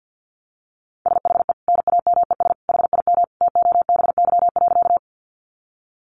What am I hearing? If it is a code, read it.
Morse code, heard as a keyed tone, about 50 wpm, 700 Hz. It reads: SHE NAMES HIM TOBY.
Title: Easy Phrases 50wpm